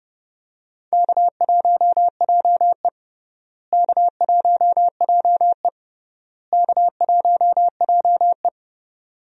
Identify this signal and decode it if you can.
Morse code, heard as a keyed tone, about 32 wpm, 700 Hz. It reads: K1JE K1JE K1JE
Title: USA Call Signs - 4 Characters - Repeated 3x - 30wpm